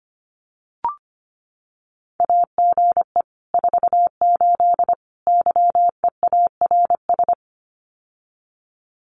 Morse code, heard as a keyed tone, about 25 wpm, 700 Hz. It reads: AGE 48 YEARS
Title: Common QSO Phrases Encore 25wpm